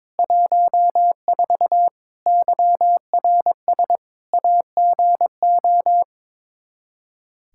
Morse code, heard as a keyed tone, about 21 wpm, 700 Hz. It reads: 14 YRS AGO
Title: K1USN SST QSO Practice 22wpm